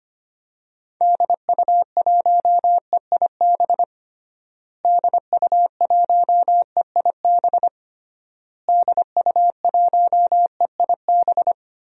Morse code, heard as a keyed tone, about 25 wpm, 700 Hz. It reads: DU1EIB DU1EIB DU1EIB
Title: DXCC-100 Call Signs - 6 Characters - Repeated 3x - 25wpm